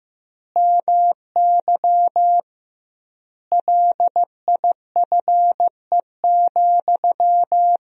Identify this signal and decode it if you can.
Morse code, heard as a keyed tone, about 15 wpm, 700 Hz. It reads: MY LIFE,
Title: 2x Word-Spacing - Sentences from Top 300 Words - 15wpm